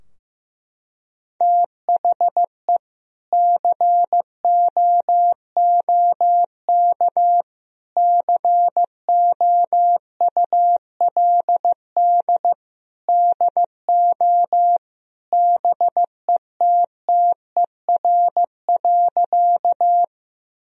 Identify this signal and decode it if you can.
Morse code, heard as a keyed tone, about 15 wpm, 700 Hz. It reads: THE COOK COULD DO BETTER.
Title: Sentences from Top 1000 Words - Review Part 4 - 15wpm